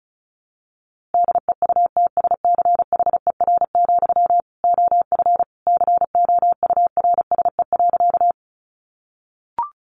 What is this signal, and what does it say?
Morse code, heard as a keyed tone, about 34 wpm, 700 Hz. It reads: DEUTSCHER, OF COURSE.
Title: A Sound Of Thunder 35wpm